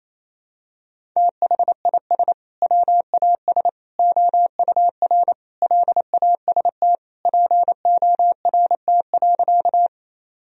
Morse code, heard as a keyed tone, about 27 wpm, 700 Hz. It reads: THIS WAS OUR LAST PORT.
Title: Sentences from Top 1000 Words - Review Part 1 - 28wpm